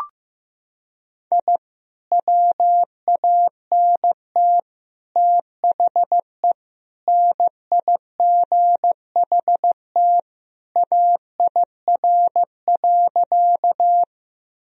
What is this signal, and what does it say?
Morse code, heard as a keyed tone, about 15 wpm, 700 Hz. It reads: I WANT THE NIGHT AIR.
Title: Sentences from Top 1000 Words - Review Part 1 - 15wpm